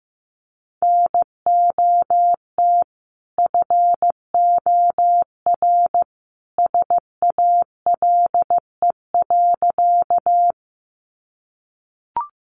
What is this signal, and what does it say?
Morse code, heard as a keyed tone, about 15 wpm, 700 Hz. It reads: NOT FOR SALE.
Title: Sentences from Top 1400 Words 15wpm